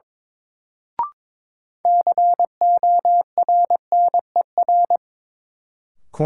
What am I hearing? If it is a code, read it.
Morse code, heard as a keyed tone, about 22 wpm, 700 Hz. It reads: CORNER
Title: Sets of 5 Words - New Word Drill - 22wpm